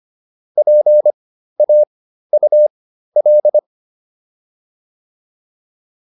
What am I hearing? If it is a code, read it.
Morse code, heard as a keyed tone, about 25 wpm, spaced as for 13 wpm, 600 Hz. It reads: PAUL
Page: intermediate-practice-files